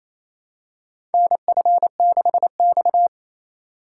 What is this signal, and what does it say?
Morse code, heard as a keyed tone, about 28 wpm, 700 Hz. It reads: NF6X
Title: USA Call Signs - 4 Characters - 3x Character Spacing - 28wpm